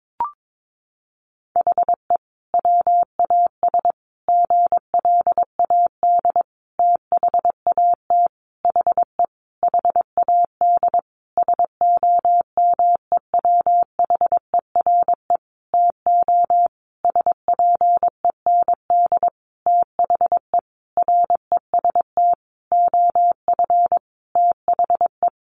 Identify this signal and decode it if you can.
Morse code, heard as a keyed tone, about 22 wpm, 700 Hz. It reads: HE WAS GLAD THAT HE HAD SOMEWHERE TO SPEND THE REST OF THE
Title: Sentences from Top 1400 Words 22wpm